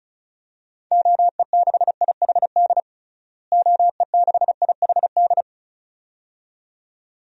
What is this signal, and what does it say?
Morse code, heard as a keyed tone, about 36 wpm, 700 Hz. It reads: OE6IHD OE6IHD
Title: DXCC-100 Call Signs - 6 Characters - Repeated 2x - 35wpm